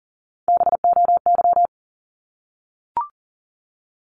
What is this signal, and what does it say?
Morse code, heard as a keyed tone, about 40 wpm, 700 Hz. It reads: BOY
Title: Top 200 Words 40wpm